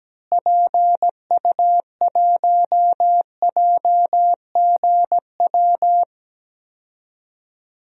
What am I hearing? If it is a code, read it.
Morse code, heard as a keyed tone, about 17 wpm, 700 Hz. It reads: PU1JGW
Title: DXCC-100 Call Signs - 6 Characters - Repeated 2x - 17wpm